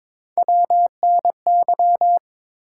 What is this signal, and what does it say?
Morse code, heard as a keyed tone, about 22 wpm, 700 Hz. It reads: WNY
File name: ARRL Sections v2 - 22wpm